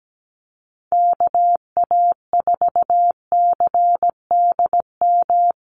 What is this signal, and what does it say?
Morse code, heard as a keyed tone, about 17 wpm, 700 Hz. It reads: KA4CDM
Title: USA Call Signs Repeated 2x 17wpm